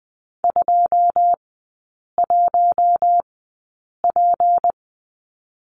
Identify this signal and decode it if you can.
Morse code, heard as a keyed tone, about 20 wpm, spaced as for 8 wpm, 700 Hz. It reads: 21P
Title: 152 Lesson 25e - Letter p - Three Characters 20wpm